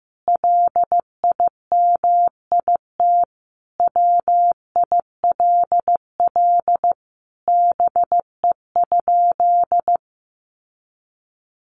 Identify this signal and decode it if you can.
Morse code, heard as a keyed tone, about 14 wpm, 700 Hz. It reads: LIMIT WILL BE?
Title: Sentences from Top 1500 Words 15wpm